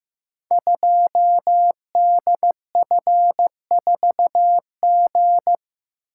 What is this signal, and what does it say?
Morse code, heard as a keyed tone, about 15 wpm, 700 Hz. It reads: 2DF4G
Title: Code Groups - 5 Letter:Numbers - 15wpm